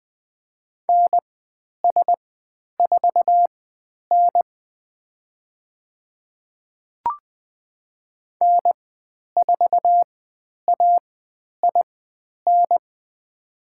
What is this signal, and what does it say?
Morse code, heard as a keyed tone, about 20 wpm, spaced as for 10 wpm, 700 Hz. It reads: NS4N N4AIN
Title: 095 Lesson 16g - Letter c - Call Signs 20wpm